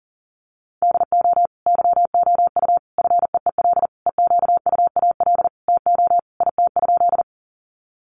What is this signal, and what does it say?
Morse code, heard as a keyed tone, about 37 wpm, 700 Hz. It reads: DO YOU FEEL EQUAL TO IT?
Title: Sentences from Top 1400 Words 40wpm